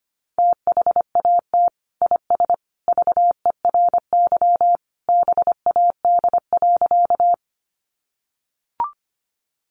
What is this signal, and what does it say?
Morse code, heard as a keyed tone, about 25 wpm, 700 Hz. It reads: THAT IS VERY BAD.
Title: Sentences from Top 700 Words 25wpm